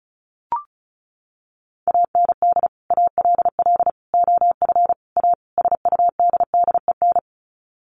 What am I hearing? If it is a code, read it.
Morse code, heard as a keyed tone, about 34 wpm, 700 Hz. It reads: AND ALL OF A SUDDEN
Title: Sets of 5 Words 35wpm